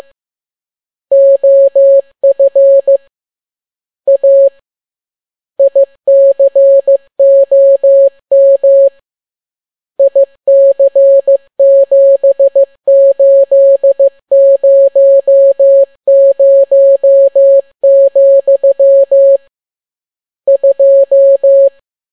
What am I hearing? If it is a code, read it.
Morse code, heard as a keyed tone, about 15 wpm, 550 Hz. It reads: OF A ICOM IC7800, 2